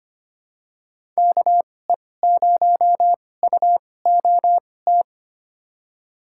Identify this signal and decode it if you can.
Morse code, heard as a keyed tone, about 25 wpm, spaced as for 18 wpm, 700 Hz. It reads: KE0UOT
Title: USA Call Signs - 6 Characters - 2x Character Spacing - 25wpm